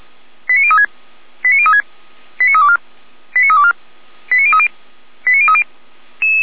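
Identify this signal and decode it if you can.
Five-tone selective call (ZVEI), heard as quick runs of five tones, tones of about 70 ms. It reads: 89036 89036 89324 89324 89030 89030
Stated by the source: ZVEI five-tone call received by SDR.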